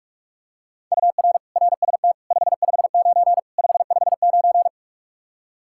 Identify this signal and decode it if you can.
Morse code, heard as a keyed tone, about 46 wpm, 700 Hz. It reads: UR RST 559 559
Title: Practice QSO - Exchange 1 - Beginning - 45wpm